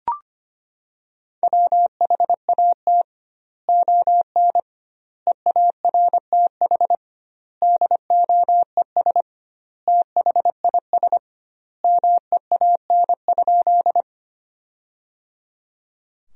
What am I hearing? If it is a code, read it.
Morse code, heard as a keyed tone, about 25 wpm, 700 Hz. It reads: WHAT ON EARTH DOES THIS MEAN?
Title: 2x Word-Spacing - Sentences from Top 200 Words - 25wpm